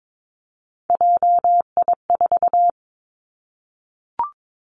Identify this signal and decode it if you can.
Morse code, heard as a keyed tone, about 22 wpm, 700 Hz. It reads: JI4
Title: Three Challenging Letter-Number Combinations 22wpm